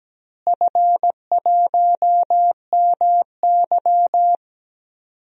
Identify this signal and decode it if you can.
Morse code, heard as a keyed tone, about 17 wpm, 700 Hz. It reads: F1MY
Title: DXCC-100 Call Signs - 4 Characters - Repeated 2x - 17wpm